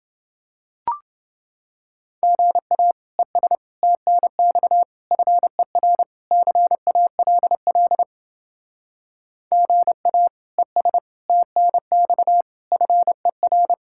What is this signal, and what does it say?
Morse code, heard as a keyed tone, about 31 wpm, 700 Hz. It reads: GA ES TNX FER CALL GA ES TNX FER